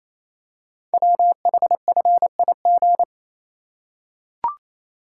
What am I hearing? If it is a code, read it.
Morse code, heard as a keyed tone, about 27 wpm, 700 Hz. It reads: WHFIG
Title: Code Groups - 5 Letters - 28wpm